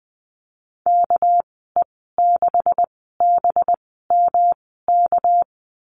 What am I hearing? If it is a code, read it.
Morse code, heard as a keyed tone, about 20 wpm, spaced as for 14 wpm, 700 Hz. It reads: KE6BMK
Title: 219 Lesson 35g - Number 0 - Call Signs 20wpm